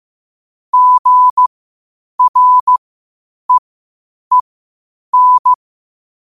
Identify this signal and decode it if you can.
Morse code, heard as a keyed tone, about 15 wpm, spaced as for 8 wpm, 1000 Hz. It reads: GREEN
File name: qotdfiles / 8wpm